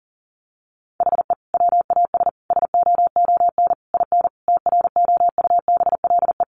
Morse code, heard as a keyed tone, about 40 wpm, 700 Hz. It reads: HE WAS SOON IN TROUBLE
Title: Sentences from Top 1200 Words 40wpm